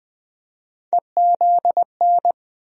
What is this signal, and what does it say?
Morse code, heard as a keyed tone, about 20 wpm, 700 Hz. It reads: EZN
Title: USA Call Sign Postfixes - 3 Letters 20wpm